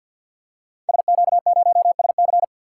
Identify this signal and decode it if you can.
Morse code, heard as a keyed tone, about 50 wpm, 700 Hz. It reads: SQ0SC
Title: Code Groups - 5 Letter:Numbers - 50wpm